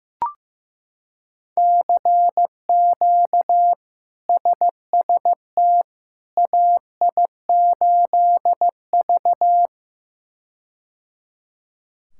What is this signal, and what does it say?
Morse code, heard as a keyed tone, about 15 wpm, 700 Hz. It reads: CQ SST AI8V
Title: K1USN SST QSO Practice - 15wpm